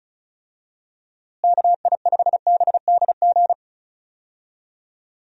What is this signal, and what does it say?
Morse code, heard as a keyed tone, about 36 wpm, 700 Hz. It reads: KI5BDG
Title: USA Call Signs - 6 Characters - Repeated 2x - 35wpm